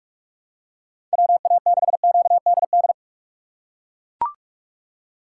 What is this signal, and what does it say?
Morse code, heard as a keyed tone, about 42 wpm, 700 Hz. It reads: WA6QDD
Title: USA Call Signs - 6 Characters - 3x Character Spacing - 45wpm